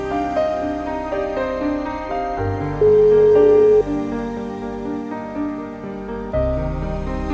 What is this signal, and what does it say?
Ringback tone (Europe, 425 Hz), with noise.